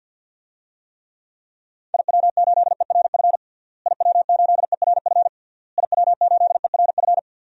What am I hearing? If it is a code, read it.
Morse code, heard as a keyed tone, about 51 wpm, 700 Hz. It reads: IW8ERF IW8ERF IW8ERF
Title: DXCC-100 Call Signs - 6 Characters - Repeated 3x - 50wpm